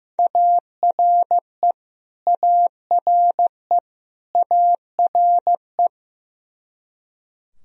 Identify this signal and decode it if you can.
Morse code, heard as a keyed tone, about 15 wpm, 700 Hz. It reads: ARE ARE ARE
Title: Top 100 Words Repeated 15wpm